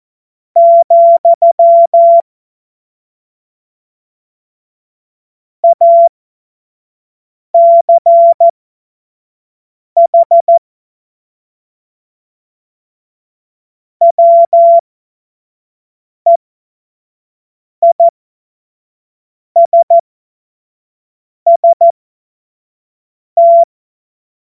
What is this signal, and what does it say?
Morse code, heard as a keyed tone, about 14 wpm, spaced as for 5 wpm, 680 Hz. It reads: , ACH WEISST
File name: Morsekurs / Witze / Witz 31.Oktober 2023 25 F70